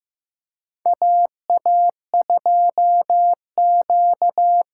Morse code, heard as a keyed tone, about 16 wpm, 700 Hz. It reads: AA2Q
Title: USA Call Signs - 5 Characters - 6x Character Spacing - 15wpm